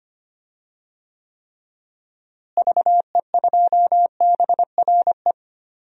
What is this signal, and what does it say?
Morse code, heard as a keyed tone, about 25 wpm, 700 Hz. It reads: VE2BRE
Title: DXCC-100 Call Signs - 6 Characters - Repeated 3x - 25wpm